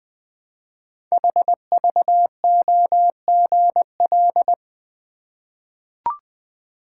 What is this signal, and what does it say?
Morse code, heard as a keyed tone, about 20 wpm, 700 Hz. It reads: HVOGL
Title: Code Groups - 5 Letter:Numbers - 20wpm